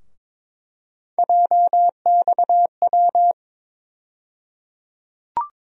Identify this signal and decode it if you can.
Morse code, heard as a keyed tone, about 22 wpm, 700 Hz. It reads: JXW
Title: USA Call Sign Postfixes - 3 Letters 22wpm